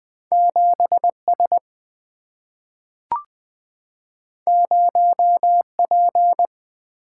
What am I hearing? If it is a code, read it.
Morse code, heard as a keyed tone, about 20 wpm, 700 Hz. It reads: 7S 0P
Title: Single-Letter Single-Number 20wpm